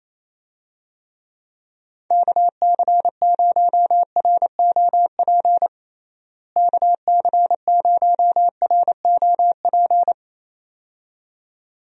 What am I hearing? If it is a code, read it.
Morse code, heard as a keyed tone, about 28 wpm, 700 Hz. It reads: KC0ROP KC0ROP
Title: USA Call Signs - 6 Characters - Repeated 2x - 28wpm